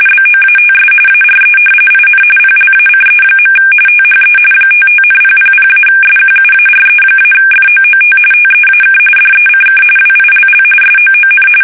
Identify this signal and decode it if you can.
RTTY, about 45.45 baud. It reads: BOATS ARE IN PLACE. THE 55,000-SQUARE-FOOT TEAK DECK WILL SOON
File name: fsk850 / battleshipnj